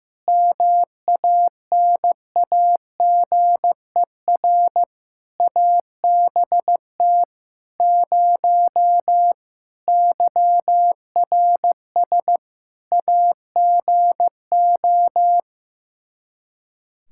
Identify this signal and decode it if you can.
Morse code, heard as a keyed tone, about 15 wpm, 700 Hz. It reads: MANAGER ABT 0 YRS AGO
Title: Practice QSO - Exchange 3 - More Info - 15wpm